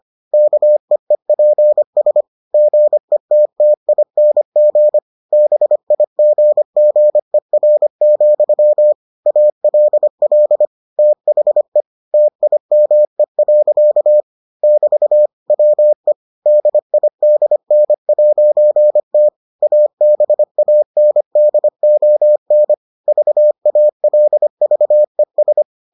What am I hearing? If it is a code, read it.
Morse code, heard as a keyed tone, about 25 wpm, 600 Hz. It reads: KEEPS GETTING BIGGER, ALL THE TIME. <BT> WE DIDN'T ABANDON VALVES